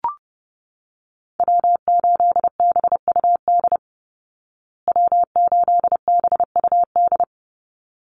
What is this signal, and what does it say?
Morse code, heard as a keyed tone, about 30 wpm, 700 Hz. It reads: W8BUD W8BUD
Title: Call Signs Repeated 3x - 30wpm